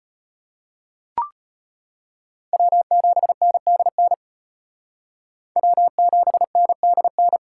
Morse code, heard as a keyed tone, about 32 wpm, 700 Hz. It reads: W7NDN W7NDN